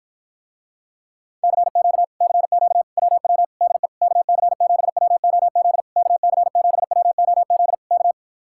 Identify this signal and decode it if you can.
Morse code, heard as a keyed tone, about 52 wpm, 700 Hz. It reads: CQ CQ WW DE KC7WOZ KC7WOZ K